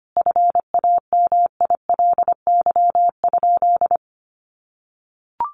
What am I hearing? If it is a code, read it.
Morse code, heard as a keyed tone, about 25 wpm, 700 Hz. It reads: FAMILY?
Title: Sentences from Top 1100 Words 25wpm